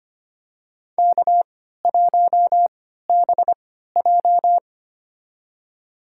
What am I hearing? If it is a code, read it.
Morse code, heard as a keyed tone, about 25 wpm, spaced as for 14 wpm, 700 Hz. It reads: K1BJ
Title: USA Call Signs - 4 Characters - 3x Character Spacing - 25wpm